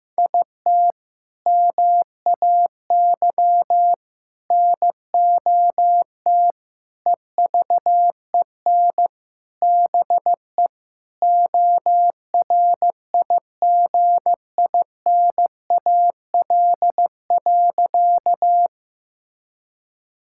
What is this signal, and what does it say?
Morse code, heard as a keyed tone, about 15 wpm, 700 Hz. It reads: IT MAY NOT EVEN BE ORIGINAL.
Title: Sentences from Top 1000 Words - Review Part 6 - 15wpm